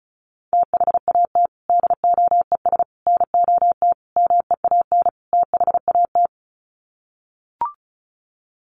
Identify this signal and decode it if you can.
Morse code, heard as a keyed tone, about 34 wpm, 700 Hz. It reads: THAT DOES NOT MEAN THAT
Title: Sets of 5 Words Encore 35wpm